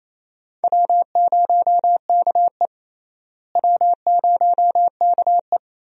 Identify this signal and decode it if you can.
Morse code, heard as a keyed tone, about 28 wpm, 700 Hz. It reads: W0KE W0KE
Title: USA Call Signs - 4 Characters - Repeated 2x - 28wpm